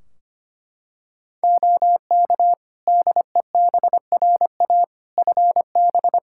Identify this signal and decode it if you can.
Morse code, heard as a keyed tone, about 25 wpm, 700 Hz. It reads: OK DEBRA FB
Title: Practice QSO - Exchange 2 - Info - 25wpm